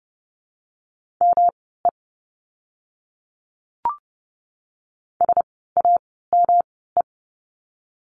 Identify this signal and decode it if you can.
Morse code, heard as a keyed tone, about 30 wpm, spaced as for 17 wpm, 700 Hz. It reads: ME SAME
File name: course / 30:10wpm / 115 Lesson 19f - Letter w - Words 30wpm